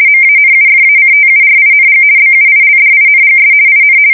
RTTY, about 50 baud. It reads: MANUFACTURING BUILDINGS IN
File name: fsk170-50B / teletype-corp